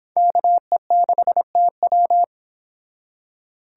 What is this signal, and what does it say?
Morse code, heard as a keyed tone, about 25 wpm, 700 Hz. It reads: KE6TW